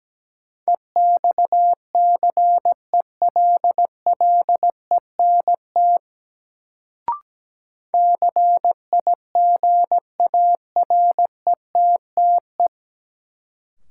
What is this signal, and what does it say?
Morse code, heard as a keyed tone, about 17 wpm, 700 Hz. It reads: EXCELLENT CIGARETTE
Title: Sentences from Top 1000 Words - New Word Drill - 17wpm